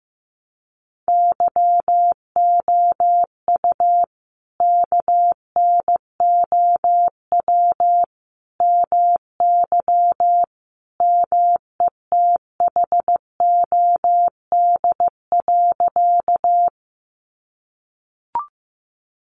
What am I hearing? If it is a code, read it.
Morse code, heard as a keyed tone, about 15 wpm, 700 Hz. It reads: YOU KNOW MY METHOD.
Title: Sentences from Top 600 Words 15wpm